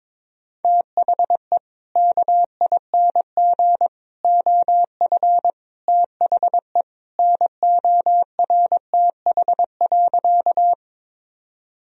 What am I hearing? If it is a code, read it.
Morse code, heard as a keyed tone, about 22 wpm, 700 Hz. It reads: THE KING OF THE NORTH.
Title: Sentences from Top 1000 Words - Review Part 1 - 22wpm